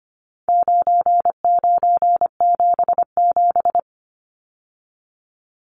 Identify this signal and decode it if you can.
Morse code, heard as a keyed tone, about 25 wpm, 700 Hz. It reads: 9977
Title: Numbers - 4-Digit 25wpm